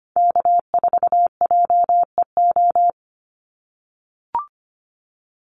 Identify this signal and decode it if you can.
Morse code, heard as a keyed tone, about 25 wpm, 690 Hz. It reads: K4JEO
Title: USA Call Signs 25wpm